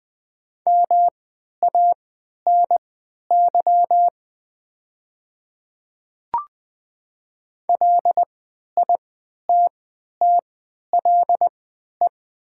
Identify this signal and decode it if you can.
Morse code, heard as a keyed tone, about 20 wpm, spaced as for 11 wpm, 700 Hz. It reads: MANY LITTLE
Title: 3x Character Spacing - Top 200 Words - 20wpm